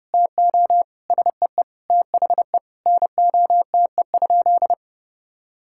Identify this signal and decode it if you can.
Morse code, heard as a keyed tone, about 32 wpm, 700 Hz. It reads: TO SEE THE NOTE?
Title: Sentences from Top 1600 Words 30wpm